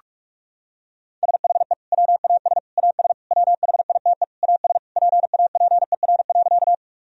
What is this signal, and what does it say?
Morse code, heard as a keyed tone, about 41 wpm, 700 Hz. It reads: SHE WAS AS WHITE AS PAPER.
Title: Sentences from Top 1000 Words - Review Part 1 - 45wpm